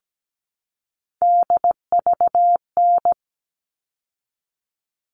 Morse code, heard as a keyed tone, about 17 wpm, 700 Hz. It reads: DVN
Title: All Cut Numbers - 3-Digit 17wpm